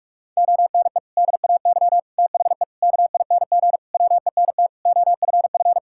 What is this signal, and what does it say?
Morse code, heard as a keyed tone, about 46 wpm, 700 Hz. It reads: ONE DAY THE KING WENT OFF
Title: Sentences from Top 1600 Words 45wpm